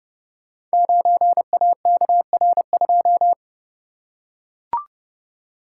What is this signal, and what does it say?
Morse code, heard as a keyed tone, about 30 wpm, 700 Hz. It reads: 9AKR2
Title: Code Groups - 5 Letter:Numbers - 30wpm